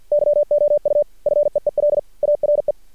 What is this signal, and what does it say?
Morse code, heard as a keyed tone, about 50 wpm, 590 Hz. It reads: YOU FEEL ARE
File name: archived / CRSnet